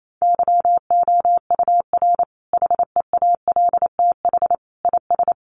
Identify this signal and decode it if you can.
Morse code, heard as a keyed tone, about 28 wpm, 700 Hz. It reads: YOUR HEALTH IS
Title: Sentences from Top 600 Words 28wpm